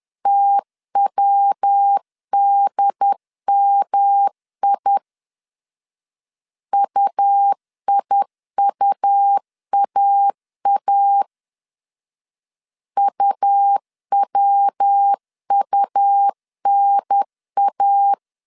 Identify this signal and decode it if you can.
Morse code, heard as a keyed tone, about 10 wpm, 800 Hz. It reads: TWDMI UIUAA UWUNA
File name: CW / DET / 1 3 1500C S B 10wpm 3335